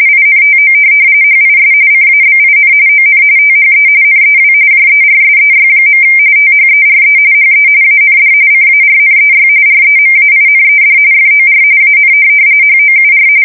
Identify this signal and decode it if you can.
RTTY, about 45.45 baud. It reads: INTO HIS CABIN ON THE NEW JERSEY. THERE HE AND CARNEY REMAINED FOR AN HOUR
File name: fsk170 / USS-New-Jersey